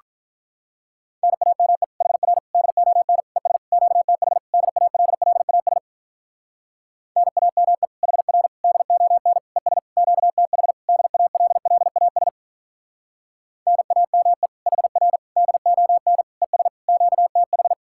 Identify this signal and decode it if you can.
Morse code, heard as a keyed tone, about 44 wpm, 700 Hz. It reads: NAME HR DON ES QTH DALLAS NAME HR DON ES QTH DALLAS NAME HR DON ES QTH